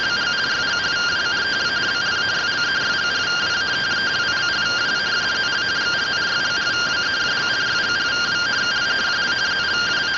Two-tone teleprinter signal, about 170 Hz shift.